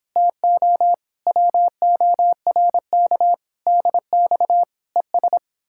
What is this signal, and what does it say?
Morse code, heard as a keyed tone, about 29 wpm, 700 Hz. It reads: TO WORK DX ES